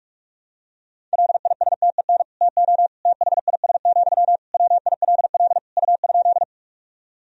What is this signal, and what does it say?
Morse code, heard as a keyed tone, about 41 wpm, 700 Hz. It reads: LISTEN TO THIS, WILL U?
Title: Sentences from Top 400 Words and Abbreviations 45wpm